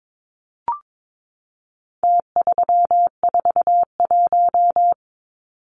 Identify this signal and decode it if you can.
Morse code, heard as a keyed tone, about 22 wpm, 700 Hz. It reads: T341
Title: Numbers - Contest Style - 4-Digit 22wpm